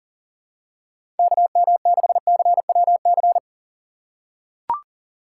Morse code, heard as a keyed tone, about 38 wpm, 700 Hz. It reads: KM6CWC
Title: USA Call Signs - 6 Characters - 3x Character Spacing - 40wpm